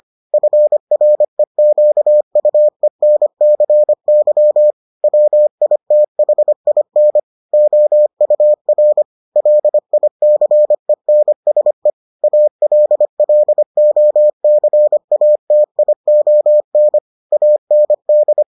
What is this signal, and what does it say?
Morse code, heard as a keyed tone, about 25 wpm, 600 Hz. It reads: FREQUENCY WITHIN OUR LICENSE ALLOCATION AND